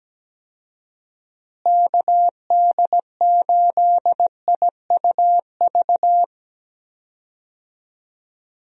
Morse code, heard as a keyed tone, about 17 wpm, 700 Hz. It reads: KD8IUV
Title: USA Call Signs - 6 Characters - Repeated 2x - 17wpm